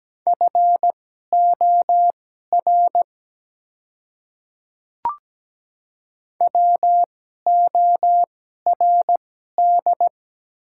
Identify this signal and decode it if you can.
Morse code, heard as a keyed tone, about 17 wpm, spaced as for 12 wpm, 700 Hz. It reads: FOR WORD
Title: 2x Character Spacing - Top 100 Words - 17wpm